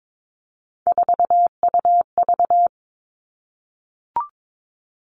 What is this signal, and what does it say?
Morse code, heard as a keyed tone, about 22 wpm, 700 Hz. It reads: 4UV
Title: Three Challenging Letter-Number Combinations 22wpm